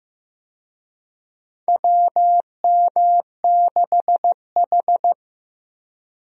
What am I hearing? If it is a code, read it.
Morse code, heard as a keyed tone, about 15 wpm, 700 Hz. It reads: WM6H
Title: USA Call Signs - 4 Characters - Repeated 2x - 15wpm